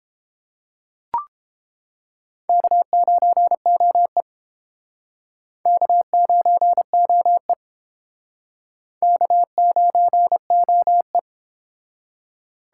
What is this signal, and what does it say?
Morse code, heard as a keyed tone, about 29 wpm, 700 Hz. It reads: K9OE K9OE K9OE